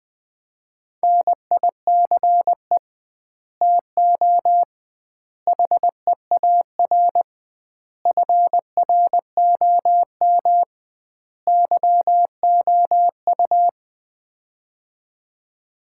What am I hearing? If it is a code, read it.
Morse code, heard as a keyed tone, about 20 wpm, 700 Hz. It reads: NICE TO HEAR FROM YOU
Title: Common QSO Phrases - 2x Word-Spacing - 20wpm